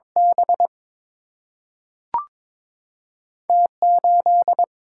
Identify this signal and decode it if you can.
Morse code, heard as a keyed tone, about 22 wpm, 700 Hz. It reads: B T 8
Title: Single-Letter Single-Number 22wpm